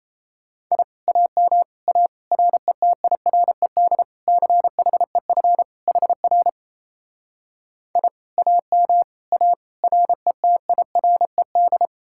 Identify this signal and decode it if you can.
Morse code, heard as a keyed tone, about 36 wpm, 700 Hz. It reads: I AM A RETIRED CHEF HR I AM A RETIRED